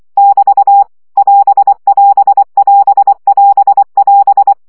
Morse code, heard as a keyed tone, about 24 wpm, 800 Hz. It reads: <BT> <AS><AS><AS><AS><AS>
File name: CW gyakorlas as 20 baud